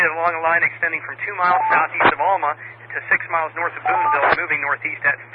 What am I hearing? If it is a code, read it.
Emergency Alert System (SAME) tones, heard as harsh buzzing bursts, with speech. It is heard from 0.9 to 4.9 s.